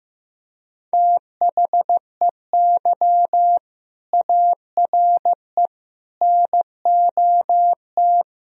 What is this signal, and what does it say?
Morse code, heard as a keyed tone, about 16 wpm, 700 Hz. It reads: THEY ARE NOT
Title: Sentences from Top 1000 Words - Review Part 3 - 15wpm